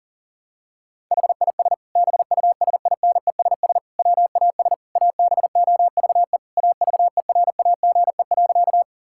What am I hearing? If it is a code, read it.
Morse code, heard as a keyed tone, about 40 wpm, 700 Hz. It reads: HIS BUSINESS WAS ABOVE AVERAGE.
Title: Sentences from Top 1000 Words - Review Part 6 - 40wpm